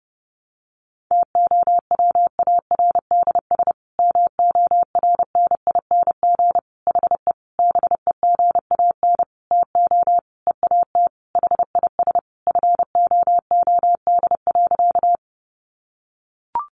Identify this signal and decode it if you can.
Morse code, heard as a keyed tone, about 30 wpm, 700 Hz. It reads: TOWARDS MORNING HE BEGAN TO EAT HIS FOOD.
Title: Sentences from Top 1100 Words 30wpm